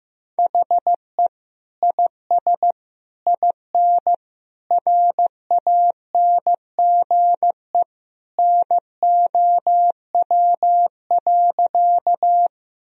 Morse code, heard as a keyed tone, about 15 wpm, 700 Hz. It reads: HE IS IN RANGE NOW.
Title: Sentences from Top 1000 Words - Review Part 4 - 15wpm